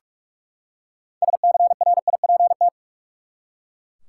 Morse code, heard as a keyed tone, about 43 wpm, 700 Hz. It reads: SCRIPT
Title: Six Letter Words - 45wpm